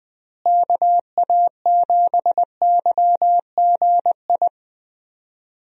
Morse code, heard as a keyed tone, about 21 wpm, 700 Hz. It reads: KA7YGI
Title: USA Call Signs - 6 Characters - Repeated 3x - 20wpm